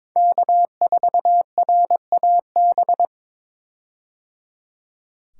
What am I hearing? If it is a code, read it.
Morse code, heard as a keyed tone, about 22 wpm, 700 Hz. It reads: K4RAB
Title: K1USN SST QSO Practice - 22wpm